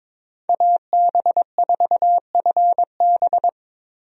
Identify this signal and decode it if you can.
Morse code, heard as a keyed tone, about 22 wpm, 700 Hz. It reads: AB4FB
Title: USA Call Signs - 5 Characters - Repeated 3x - 22wpm